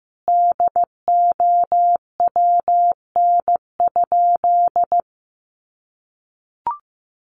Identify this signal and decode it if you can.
Morse code, heard as a keyed tone, about 15 wpm, 700 Hz. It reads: DOWN?
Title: Sentences from Top 1300 Words 15wpm